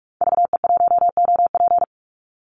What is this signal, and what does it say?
Morse code, heard as a keyed tone, about 44 wpm, 700 Hz. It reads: VE1OP
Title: Call Signs Repeated 2x - 45wpm